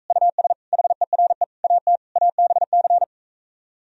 Morse code, heard as a keyed tone, about 45 wpm, 700 Hz. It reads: US HERE AT ABC